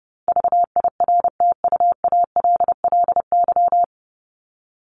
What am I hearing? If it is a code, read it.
Morse code, heard as a keyed tone, about 30 wpm, 700 Hz. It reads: VIRTUALLY
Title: LY Suffix - Top 100 Words with Base Words 30wpm